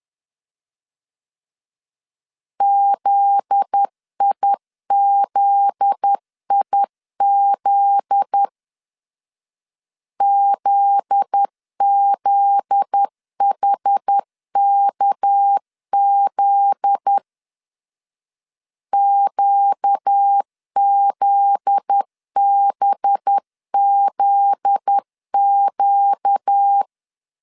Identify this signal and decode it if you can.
Morse code, heard as a keyed tone, about 10 wpm, 800 Hz. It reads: ZIZIZ ZZHKZ QZBZQ
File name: LEZIONI / 7 GF9Z / 7 GF9Z 10wpm 2611